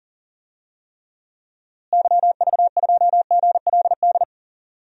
Morse code, heard as a keyed tone, about 40 wpm, 700 Hz. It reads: YV2GLD
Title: DXCC-100 Call Signs - 6 Characters - Repeated 2x - 40wpm